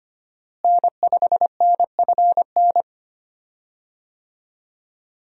Morse code, heard as a keyed tone, about 25 wpm, 700 Hz. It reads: N5NFN
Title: USA Call Signs - 5 Characters - Repeated 3x - 25wpm